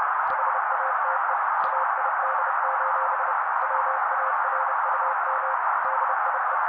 Morse code, heard as a keyed tone, about 30 wpm, 530 Hz. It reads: HATE AID: WARUM DIE
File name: daily / 29wpm noise